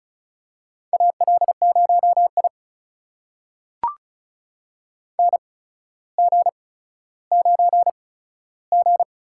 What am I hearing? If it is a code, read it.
Morse code, heard as a keyed tone, about 36 wpm, spaced as for 15 wpm, 700 Hz. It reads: AL0I NG9G
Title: USA Call Signs - 4 Characters - 8x Character Spacing - 35wpm